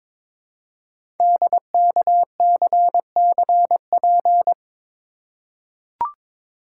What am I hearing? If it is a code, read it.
Morse code, heard as a keyed tone, about 22 wpm, 690 Hz. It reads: DKCCP
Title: Code Groups - 5 Letters - 22wpm